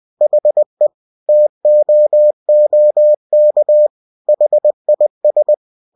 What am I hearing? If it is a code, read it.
Morse code, heard as a keyed tone, about 20 wpm, 600 Hz. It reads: HE TOOK HIS
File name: cw-books / winnie-the-pooh / winnie-20wpm-page-0095